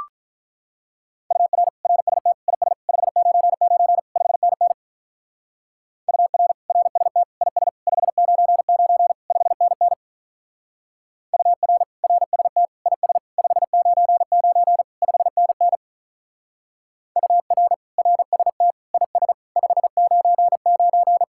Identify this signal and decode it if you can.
Morse code, heard as a keyed tone, about 43 wpm, 700 Hz. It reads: UR RST IS 599 5NN UR RST IS 599 5NN UR RST IS 599 5NN UR RST IS 599
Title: Speed-Racing Common QSO Phrases - 35wpm